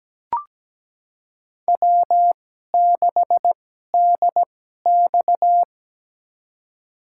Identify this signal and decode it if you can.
Morse code, heard as a keyed tone, about 17 wpm, spaced as for 12 wpm, 700 Hz. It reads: W6DX
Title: USA Call Signs - 4 Characters - 2x Character Spacing - 17wpm